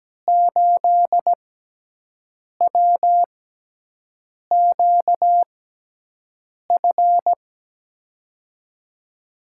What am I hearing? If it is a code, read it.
Morse code, heard as a keyed tone, about 17 wpm, spaced as for 6 wpm, 700 Hz. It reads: 8WQF
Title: USA Call Signs - 6x Character Spacing - 17wpm